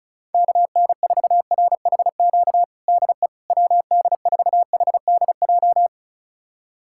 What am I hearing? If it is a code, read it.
Morse code, heard as a keyed tone, about 34 wpm, 700 Hz. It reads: KN4RHQ DE WD4HDJ
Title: USA Callsign DE Callsign - 6 Characters - 35wpm